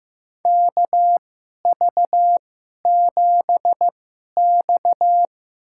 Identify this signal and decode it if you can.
Morse code, heard as a keyed tone, about 16 wpm, spaced as for 11 wpm, 700 Hz. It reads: KV7X
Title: USA Call Signs - 4 Characters - 2x Character Spacing - 15wpm